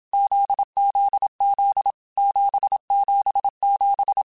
Morse code, heard as a keyed tone, about 27 wpm, 800 Hz. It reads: ZZZ 777
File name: Groups@25 / Groups@25 mp3s / JZ17@25